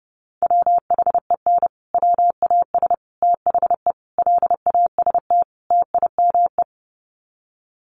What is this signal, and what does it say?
Morse code, heard as a keyed tone, about 30 wpm, 700 Hz. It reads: WHEN WAS THE LAST TIME
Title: Sets of 5 Words Encore 30wpm